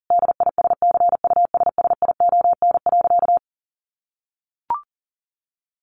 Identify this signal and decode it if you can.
Morse code, heard as a keyed tone, about 40 wpm, 700 Hz. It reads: DISCUSSION.
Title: Sentences from Top 1100 Words 40wpm